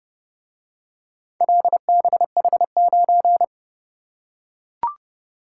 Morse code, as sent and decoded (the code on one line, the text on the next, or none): .-.. -... .... ----.
LBH9